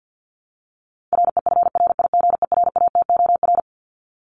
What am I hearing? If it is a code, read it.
Morse code, heard as a keyed tone, about 50 wpm, 700 Hz. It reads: REFRIGERATOR
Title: Top 1501-1750 Words 50wpm